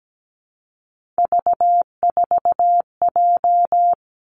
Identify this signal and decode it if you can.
Morse code, heard as a keyed tone, about 17 wpm, 700 Hz. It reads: V4J
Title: Three Challenging Letter-Number Combinations 17wpm